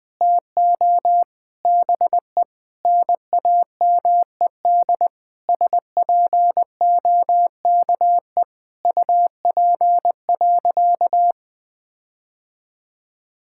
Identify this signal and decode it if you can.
Morse code, heard as a keyed tone, about 21 wpm, 700 Hz. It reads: TO BE NAMED SPOKE UP.
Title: Sentences with 6-Word Phrases - Spoken Beginning - 20wpm